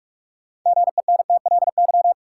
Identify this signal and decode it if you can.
Morse code, heard as a keyed tone, about 46 wpm, 700 Hz. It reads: GENTLY
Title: Sentences from Top 1500 Words - New Word Drill Alt - 45wpm